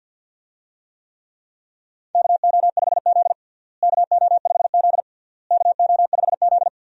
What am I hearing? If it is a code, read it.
Morse code, heard as a keyed tone, about 47 wpm, 700 Hz. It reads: KO5Z KO5Z KO5Z
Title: USA Call Signs - 4 Characters - Repeated 3x - 50wpm